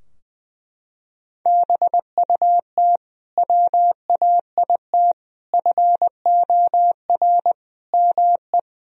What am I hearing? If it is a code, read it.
Morse code, heard as a keyed tone, about 20 wpm, 700 Hz. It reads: BUT WAIT FOR ME
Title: Sentences from Top 1000 Words - Review Part 2 - 20wpm